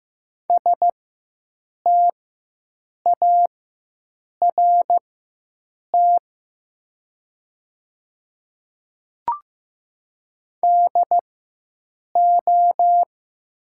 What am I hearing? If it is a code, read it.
Morse code, heard as a keyed tone, about 15 wpm, spaced as for 7 wpm, 700 Hz. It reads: START DO
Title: 4x Character Spacing - Top 500 Words - 15wpm